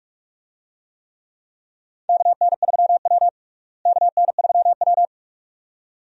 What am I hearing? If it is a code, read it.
Morse code, heard as a keyed tone, about 46 wpm, 700 Hz. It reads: KN3W KN3W
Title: USA Call Signs - 4 Characters - Repeated 2x - 45wpm